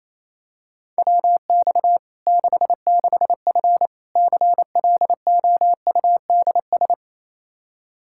Morse code, heard as a keyed tone, about 28 wpm, 700 Hz. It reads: WX 66F CLOUDS
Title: Practice QSO - Exchange 2 - Info - 28wpm